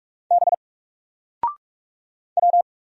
Morse code, heard as a keyed tone, about 44 wpm, 700 Hz. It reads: B W
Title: Single Character (Letter:Number:Punctuation:Prosign) 45wpm